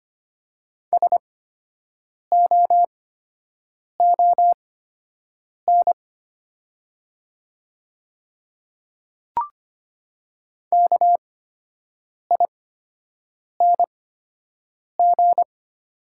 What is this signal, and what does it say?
Morse code, heard as a keyed tone, about 25 wpm, spaced as for 7 wpm, 690 Hz. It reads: SOON KING
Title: 8x Character Spacing - Top 500 Words - 25wpm